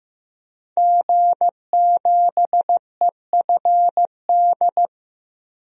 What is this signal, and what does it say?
Morse code, heard as a keyed tone, about 16 wpm, 700 Hz. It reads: G7EFD
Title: DXCC-100 Call Signs - 5 Characters - 15wpm